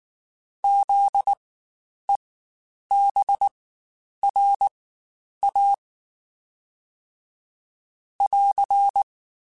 Morse code, heard as a keyed tone, about 19 wpm, spaced as for 9 wpm, 800 Hz. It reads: ZEBRA <AR>
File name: Spaced Alphas / Spaced Alphas mp3s / Zs sp @18